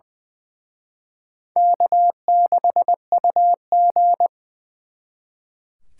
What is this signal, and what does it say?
Morse code, heard as a keyed tone, about 23 wpm, 700 Hz. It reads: K6UG
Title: Speed Racing - USA Call Signs 20wpm